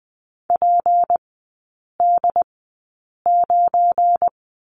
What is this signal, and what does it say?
Morse code, heard as a keyed tone, about 20 wpm, spaced as for 8 wpm, 700 Hz. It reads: PD9
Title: 218 Lesson 35e - Number 0 - Three Characters 20wpm